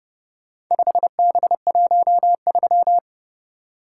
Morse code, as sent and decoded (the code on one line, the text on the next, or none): ..... -... .---- ...--
5B13